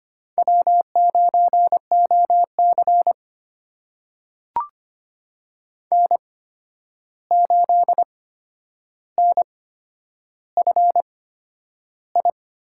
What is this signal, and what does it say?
Morse code, heard as a keyed tone, about 25 wpm, spaced as for 10 wpm, 700 Hz. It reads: W9OC N8NFI
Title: USA Call Signs - 8x Character Spacing - 25wpm